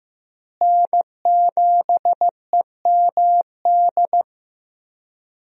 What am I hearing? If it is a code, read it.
Morse code, heard as a keyed tone, about 15 wpm, 700 Hz. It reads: N7EMD
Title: USA Call Signs - 5 Characters - Repeated 3x - 15wpm